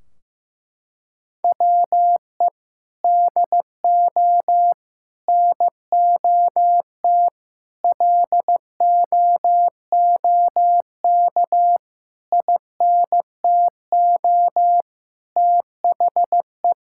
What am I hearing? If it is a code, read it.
Morse code, heard as a keyed tone, about 15 wpm, 700 Hz. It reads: WE DO NOT LOOK INTO THE
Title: Sentences from Top 1600 Words 15wpm